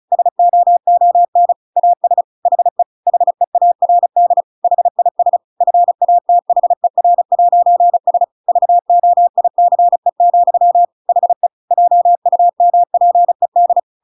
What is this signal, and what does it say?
Morse code, heard as a keyed tone, about 34 wpm, 700 Hz. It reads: SOON AS HE HEARD HIS FATHER'S VOICE, HE JUMPED
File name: Pinocchio / Pinocchio ch7 35 35 ik1hge